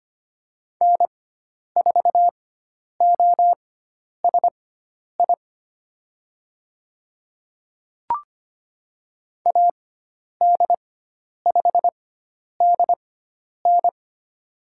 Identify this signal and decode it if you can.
Morse code, heard as a keyed tone, about 25 wpm, spaced as for 10 wpm, 700 Hz. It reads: N4OSI AD5DN
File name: Lesson 16g - Letter c - Call Signs 25wpm